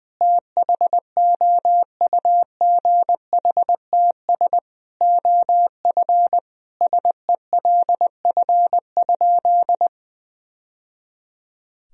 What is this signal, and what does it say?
Morse code, heard as a keyed tone, about 20 wpm, 700 Hz. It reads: THOUGHTS OF SELF?
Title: Sentences from Top 1000 Words - Review Part 1 - 20wpm